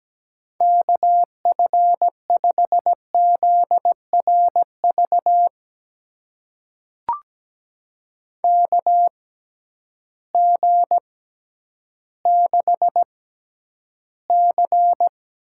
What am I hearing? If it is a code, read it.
Morse code, heard as a keyed tone, about 17 wpm, spaced as for 10 wpm, 700 Hz. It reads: KF5ZRV KG6C
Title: USA Call Signs - 6 Characters - 6x Character Spacing - 17wpm